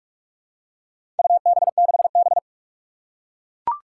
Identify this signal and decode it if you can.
Morse code, heard as a keyed tone, about 44 wpm, 700 Hz. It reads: UB6B
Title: Four Challenging Letter-Number Combinations 45wpm